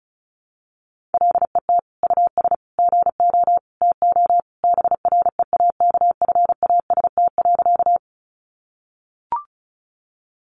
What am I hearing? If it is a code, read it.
Morse code, heard as a keyed tone, about 34 wpm, 700 Hz. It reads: LET US GO TO BREAKFAST.
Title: Sentences from Top 1500 Words 35wpm